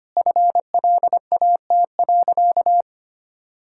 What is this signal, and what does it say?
Morse code, heard as a keyed tone, about 25 wpm, 690 Hz. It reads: FLAT.
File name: Easy Sentences with Hints - 25wpm